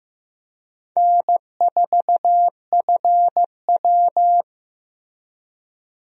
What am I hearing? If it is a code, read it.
Morse code, heard as a keyed tone, about 15 wpm, 700 Hz. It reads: N4FW
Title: USA Call Signs - 4 Characters - Repeated 3x - 15wpm